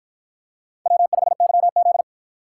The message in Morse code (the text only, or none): W5YZ